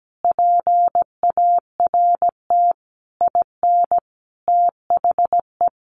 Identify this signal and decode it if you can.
Morse code, heard as a keyed tone, about 17 wpm, 700 Hz. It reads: PART IN THE
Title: Sentences from Top 1100 Words 17wpm